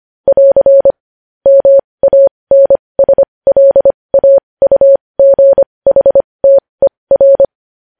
Morse code, heard as a keyed tone, about 25 wpm, spaced as for 20 wpm, 550 Hz. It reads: <AR> MANSLAUGHTER
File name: podcast / News headlines-1x-25x20